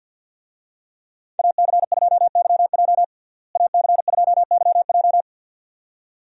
Morse code, heard as a keyed tone, about 52 wpm, 700 Hz. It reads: AC2YJ AC2YJ
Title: USA Call Signs - 5 Characters - Repeated 2x - 50wpm